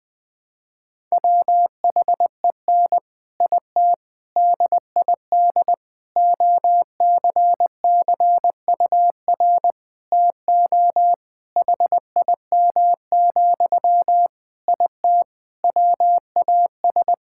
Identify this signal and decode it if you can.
Morse code, heard as a keyed tone, about 20 wpm, 700 Hz. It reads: WHEN IT DID OCCUR TO HIM, IT WAS
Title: Sentences from Top 1000 Words - Review Part 6 - 20wpm